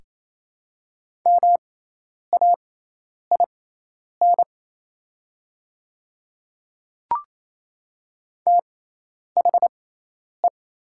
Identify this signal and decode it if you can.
Morse code, heard as a keyed tone, about 27 wpm, spaced as for 10 wpm, 700 Hz. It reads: MAIN THE
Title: 6x Character Spacing - Top 300 Words - 28wpm